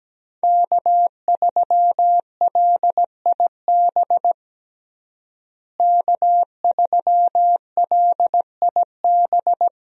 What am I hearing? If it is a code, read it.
Morse code, heard as a keyed tone, about 17 wpm, 700 Hz. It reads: K3LIB K3LIB
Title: USA Call Signs - 5 Characters - Repeated 3x - 17wpm